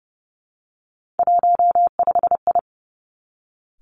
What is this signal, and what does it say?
Morse code, heard as a keyed tone, about 32 wpm, 700 Hz. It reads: 15I
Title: Three Challenging Letter-Number Combinations 30wpm